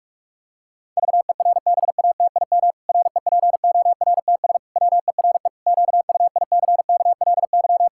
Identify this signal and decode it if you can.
Morse code, heard as a keyed tone, about 46 wpm, 700 Hz. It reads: VERBATIM REPORTS WERE QUICKLY
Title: Pangram Challenge v2 - 45wpm